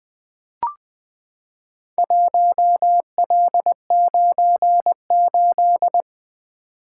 Morse code, heard as a keyed tone, about 20 wpm, 700 Hz. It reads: 1L98
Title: Four Letter-Number Combinations - 20wpm